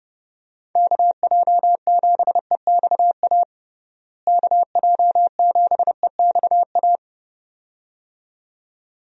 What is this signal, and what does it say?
Morse code, heard as a keyed tone, about 32 wpm, 700 Hz. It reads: KJ7EXA KJ7EXA
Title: USA Call Signs - 6 Characters - Repeated 2x - 30wpm